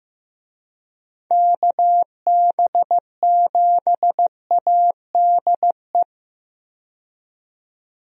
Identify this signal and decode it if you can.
Morse code, heard as a keyed tone, about 15 wpm, 700 Hz. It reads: KB7ADE
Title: USA Call Signs - 6 Characters - 15wpm